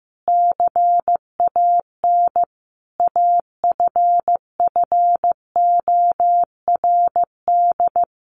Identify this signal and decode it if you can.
Morse code, heard as a keyed tone, about 14 wpm, 700 Hz. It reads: CAN AFFORD
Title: Sentences from Top 1100 Words 15wpm